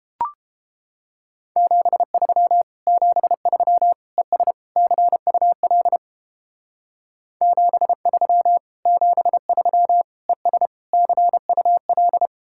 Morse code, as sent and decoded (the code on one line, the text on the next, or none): --... ...-- / --... ...-- / . ... / -.-. ..- .-.. / --... ...-- / --... ...-- / . ... / -.-. ..- .-..
73 73 ES CUL 73 73 ES CUL